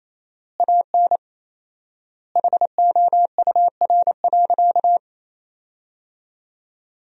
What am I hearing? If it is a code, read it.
Morse code, heard as a keyed tone, about 27 wpm, 700 Hz. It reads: AN HOUR.
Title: 4x Word-Spacing - Sentences from Top 500 Words - 28wpm